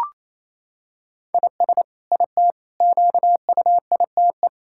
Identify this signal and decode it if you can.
Morse code, heard as a keyed tone, about 28 wpm, 700 Hz. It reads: IS IT QUITE
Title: Sentences from Top 1000 Words - Review Part 6 - 28wpm